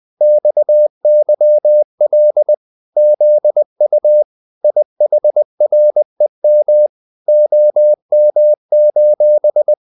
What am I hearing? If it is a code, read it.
Morse code, heard as a keyed tone, about 20 wpm, 600 Hz. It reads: XYL ZU IHREM OM:
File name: dk5ke / Humor-100BpM